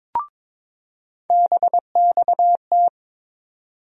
Morse code, heard as a keyed tone, about 22 wpm, 700 Hz. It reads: BXT
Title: USA Call Sign Postfixes - 3 Letters 22wpm